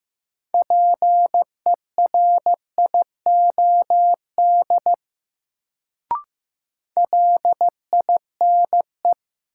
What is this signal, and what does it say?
Morse code, heard as a keyed tone, about 15 wpm, 700 Hz. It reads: PERIOD LINE
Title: Sets of 5 Words - New Word Drill - 15wpm